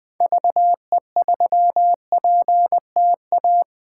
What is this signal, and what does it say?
Morse code, heard as a keyed tone, about 20 wpm, 700 Hz. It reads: VE3PTA
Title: 2023 POTA Hunter Call Signs - 20wpm